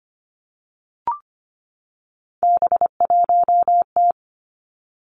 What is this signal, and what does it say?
Morse code, heard as a keyed tone, about 25 wpm, 700 Hz. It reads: B1T
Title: Three Challenging Letter-Number Combinations 25wpm